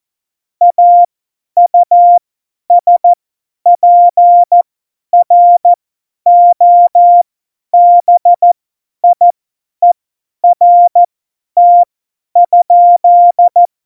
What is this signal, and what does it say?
Morse code, heard as a keyed tone, about 14 wpm, spaced as for 10 wpm, 700 Hz. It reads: AUSPROBIERT?
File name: Morsekurs / Witze / Witz 29. Januar 2019, 50 F70